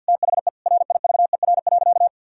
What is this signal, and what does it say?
Morse code, heard as a keyed tone, about 51 wpm, 700 Hz. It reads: THE RIVER.
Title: Sentences from Top 1600 Words 50wpm